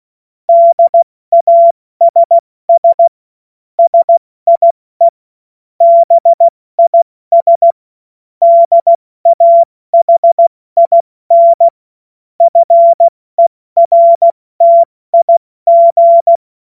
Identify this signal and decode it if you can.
Morse code, heard as a keyed tone, about 16 wpm, spaced as for 14 wpm, 680 Hz. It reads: DASS SIE BIS DAHIN FERTIG
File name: Morsekurs / Witze / Witz 14.November 2023 70 F80